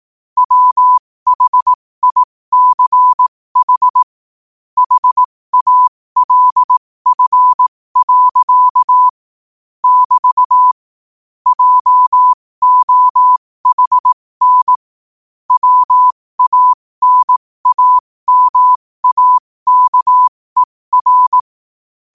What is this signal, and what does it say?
Morse code, heard as a keyed tone, about 18 wpm, spaced as for 15 wpm, 1000 Hz. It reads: WHICH HALF. <BT> JOHN WANAMAKER
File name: qotdfiles / 15wpm